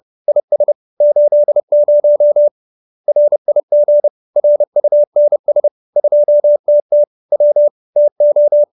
Morse code, heard as a keyed tone, about 30 wpm, 590 Hz. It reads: IS 80 RIG RUNS 2TT W TO